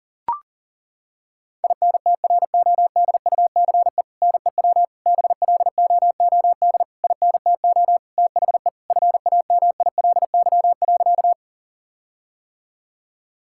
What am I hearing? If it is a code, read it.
Morse code, heard as a keyed tone, about 40 wpm, 700 Hz. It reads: INTRODUCE NEW BLOOD INTO THE FAMILY.
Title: Sentences from Top 1600 Words 40wpm